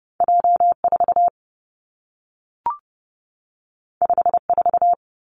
Morse code, heard as a keyed tone, about 30 wpm, 700 Hz. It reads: J 4 5 4
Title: Two Challenging Letter-Number Combinations 30wpm